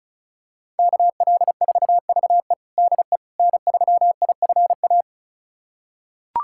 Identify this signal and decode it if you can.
Morse code, heard as a keyed tone, about 34 wpm, 700 Hz. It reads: KL4VE DE N3IFA
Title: USA Callsign DE Callsign - 5 Characters - 35wpm